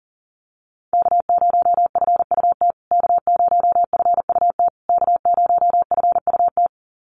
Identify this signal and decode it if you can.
Morse code, heard as a keyed tone, about 40 wpm, 700 Hz. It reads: K0FUT K0FUT K0FUT
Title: USA Call Signs Repeated 3x 40wpm